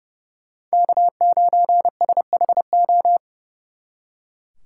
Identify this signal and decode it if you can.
Morse code, heard as a keyed tone, about 35 wpm, 700 Hz. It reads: K9SHO
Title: Speed Racing - USA 1x3 Call Signs 30wpm